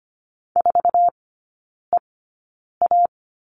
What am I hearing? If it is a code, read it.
Morse code, heard as a keyed tone, about 25 wpm, spaced as for 9 wpm, 700 Hz. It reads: 4EA
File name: Lesson 15e - Number 5 - Three Characters 25wpm